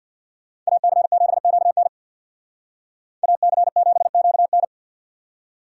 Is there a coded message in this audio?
Yes: Morse code from 0.7 to 4.6 s, heard as a keyed tone, about 64 wpm, 700 Hz. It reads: AC7QN AC7QN